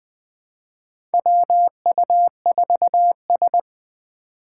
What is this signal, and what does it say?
Morse code, heard as a keyed tone, about 21 wpm, 700 Hz. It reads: WU4S
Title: USA Call Signs - 4 Characters - 2x Character Spacing - 20wpm